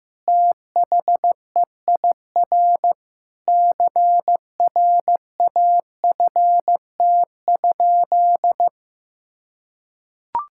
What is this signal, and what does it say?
Morse code, heard as a keyed tone, about 16 wpm, 700 Hz. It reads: THEIR CRAFT?
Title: Sentences from Top 1000 Words - Review Part 6 - 15wpm